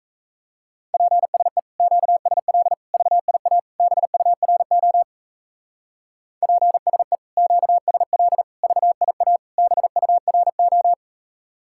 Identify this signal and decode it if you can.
Morse code, heard as a keyed tone, about 38 wpm, 700 Hz. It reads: PSE QSL VIA BURO PSE QSL VIA BURO